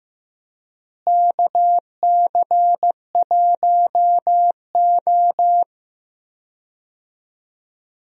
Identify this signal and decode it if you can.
Morse code, heard as a keyed tone, about 16 wpm, 700 Hz. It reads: KC1O
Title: USA Call Signs - 4 Characters - Repeated 2x - 15wpm